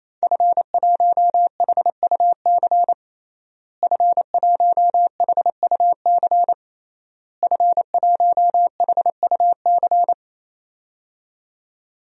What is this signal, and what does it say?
Morse code, heard as a keyed tone, about 28 wpm, 700 Hz. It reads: F1HUC F1HUC F1HUC
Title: DXCC-100 Call Signs - 5 Characters - Repeated 3x - 28wpm